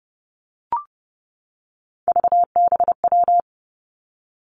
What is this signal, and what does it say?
Morse code, heard as a keyed tone, about 30 wpm, 700 Hz. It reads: VBW
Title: Three Challenging Letter-Number Combinations 30wpm